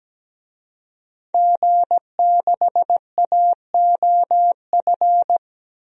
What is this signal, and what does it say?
Morse code, heard as a keyed tone, about 17 wpm, 700 Hz. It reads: G6AOF
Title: DXCC-100 Call Signs - 5 Characters - 17wpm